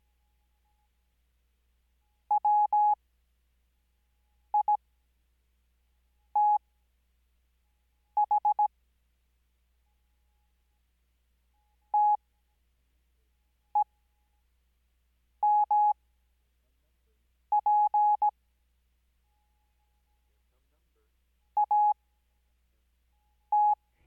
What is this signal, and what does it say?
Morse code, heard as a keyed tone, about 17 wpm, spaced as for 5 wpm, 840 Hz. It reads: WITH TEMP AT
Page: Farnsworth Spacing: 15WPM Character speed Spaced at 5WPM.